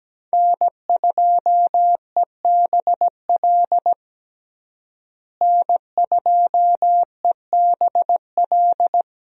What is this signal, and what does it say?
Morse code, heard as a keyed tone, about 17 wpm, 700 Hz. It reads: N2EBL N2EBL
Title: USA Call Signs - 5 Characters - Repeated 3x - 17wpm